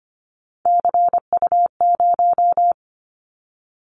Morse code, heard as a keyed tone, about 25 wpm, 700 Hz. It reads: CU0
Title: Callsign Prefixes for Top 50 DXCC - 25wpm